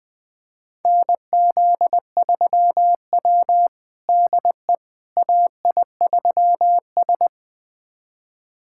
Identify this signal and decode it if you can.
Morse code, heard as a keyed tone, about 21 wpm, 700 Hz. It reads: NZ3W DE AI3S
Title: USA Callsign DE Callsign - 4 Characters - 20wpm